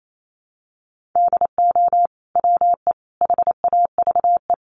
Morse code, heard as a keyed tone, about 28 wpm, 700 Hz. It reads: DO WE HAVE
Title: Sentences from Top 1400 Words 28wpm